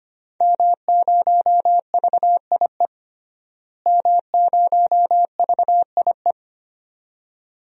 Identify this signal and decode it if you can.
Morse code, heard as a keyed tone, about 25 wpm, 700 Hz. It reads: M0VIE M0VIE
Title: DXCC-100 Call Signs - 5 Characters - Repeated 2x - 25wpm